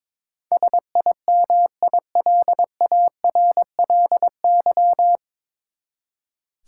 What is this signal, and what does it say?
Morse code, heard as a keyed tone, about 22 wpm, 700 Hz. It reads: SIMILARLY
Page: Morse Code Ninja practice 22wpm